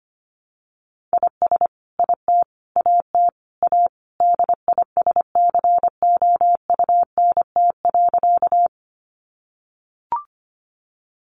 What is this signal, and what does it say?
Morse code, heard as a keyed tone, about 25 wpm, 700 Hz. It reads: IS IT AT A DISCOUNT.
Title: Sentences from Top 1200 Words 25wpm